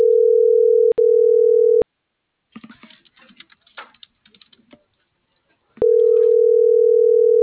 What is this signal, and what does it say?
Ringback tone (North America), 440 + 480 Hz.